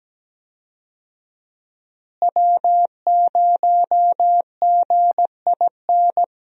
Morse code, heard as a keyed tone, about 17 wpm, 700 Hz. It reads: W0GIN
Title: USA Call Signs - 5 Characters - Repeated 2x - 17wpm